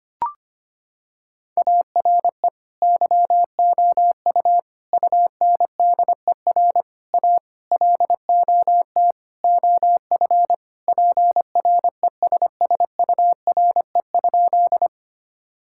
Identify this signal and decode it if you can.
Morse code, heard as a keyed tone, about 25 wpm, 700 Hz. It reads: ARE YOU UNDER A LOT OF PRESSURE?
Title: Sentences from Top 1000 Words - Review Part 6 - 25wpm